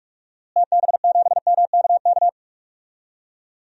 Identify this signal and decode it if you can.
Morse code, heard as a keyed tone, about 46 wpm, 690 Hz. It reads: TB7MKK
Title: DXCC-100 Call Signs - 6 Characters - Repeated 2x - 45wpm